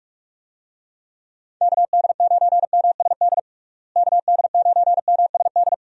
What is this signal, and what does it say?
Morse code, heard as a keyed tone, about 42 wpm, 700 Hz. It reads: KD9MSD KD9MSD
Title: USA Call Signs - 6 Characters - Repeated 2x - 45wpm